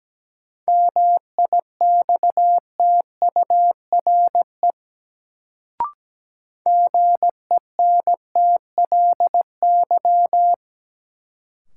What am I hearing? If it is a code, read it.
Morse code, heard as a keyed tone, about 17 wpm, 700 Hz. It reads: MIXTURE GENTLY
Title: Sentences from Top 1500 Words - New Word Drill - 17wpm